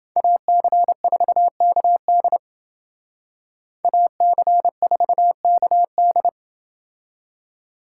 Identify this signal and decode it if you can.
Morse code, heard as a keyed tone, about 28 wpm, 700 Hz. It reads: AC4KD AC4KD